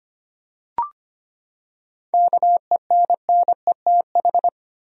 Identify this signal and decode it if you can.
Morse code, heard as a keyed tone, about 25 wpm, 700 Hz. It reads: KENNETH
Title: USA Common Names - 25wpm